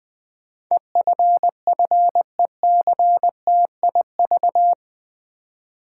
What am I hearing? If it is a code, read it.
Morse code, heard as a keyed tone, about 20 wpm, 700 Hz. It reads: EFFECTIV
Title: Build A Word - 4 Letters At A Time - 11-Letter Words - 20wpm